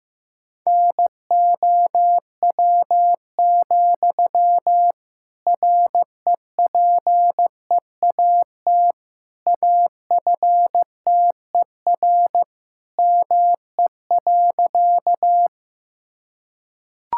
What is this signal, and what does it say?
Morse code, heard as a keyed tone, about 15 wpm, 700 Hz. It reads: NOW, REPEAT AFTER ME.
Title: Sentences from Top 1600 Words 15wpm